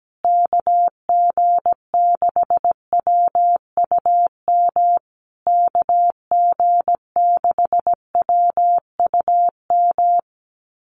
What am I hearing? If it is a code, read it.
Morse code, heard as a keyed tone, about 17 wpm, 700 Hz. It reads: KG6WUM KG6WUM
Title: USA Call Signs Repeated 3x 17wpm